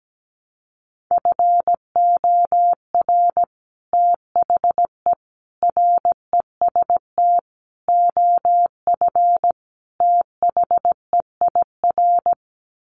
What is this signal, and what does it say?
Morse code, heard as a keyed tone, about 17 wpm, 700 Hz. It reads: FOR THE REST OF THEIR
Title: Sets of 5 Words Encore2 17wpm